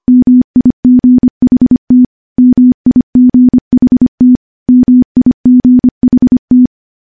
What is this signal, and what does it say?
Morse code, heard as a keyed tone, about 25 wpm, 250 Hz. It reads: MIGHT MIGHT MIGHT
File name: morse / morse-training-25WPM